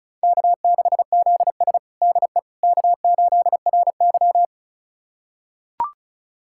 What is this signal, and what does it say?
Morse code, heard as a keyed tone, about 36 wpm, 700 Hz. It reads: K6ZS DE K8RY
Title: USA Callsign DE Callsign - 4 Characters - 35wpm